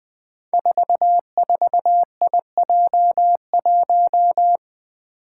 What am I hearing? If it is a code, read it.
Morse code, heard as a keyed tone, about 20 wpm, 700 Hz. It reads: 44IJ1
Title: Five Challenging Letter-Number Combinations 20wpm